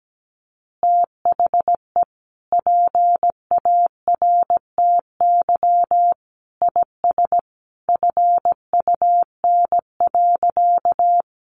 Morse code, heard as a keyed tone, about 17 wpm, 700 Hz. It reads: THE PARTY IS FUN.
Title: Easy Phrases 17wpm